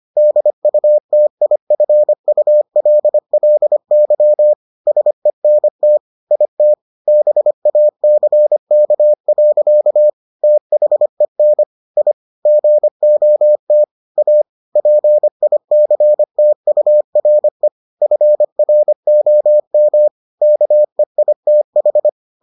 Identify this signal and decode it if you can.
Morse code, heard as a keyed tone, about 25 wpm, 600 Hz. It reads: DUTIFULLY SENT IT BACK. THEN I GOT A PICTURE FROM KEITH